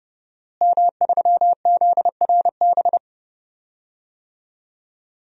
Morse code, heard as a keyed tone, about 30 wpm, 700 Hz. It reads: M3ZRB
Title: DXCC-100 Call Signs - 5 Characters - Repeated 3x - 30wpm